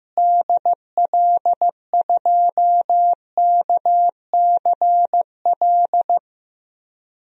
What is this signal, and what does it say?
Morse code, heard as a keyed tone, about 15 wpm, 700 Hz. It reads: DL2KCL
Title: DXCC-100 Call Signs - 6 Characters - Repeated 3x - 15wpm